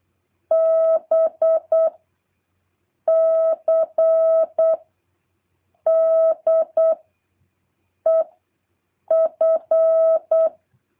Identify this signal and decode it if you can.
Morse code, heard as a keyed tone, about 8 wpm, 650 Hz. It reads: B C D E F